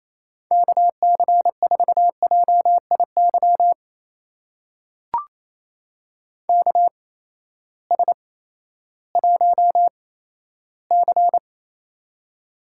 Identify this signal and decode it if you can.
Morse code, heard as a keyed tone, about 28 wpm, spaced as for 14 wpm, 700 Hz. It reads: KC4JIY KS1C
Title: USA Call Signs - 8x Character Spacing - 28wpm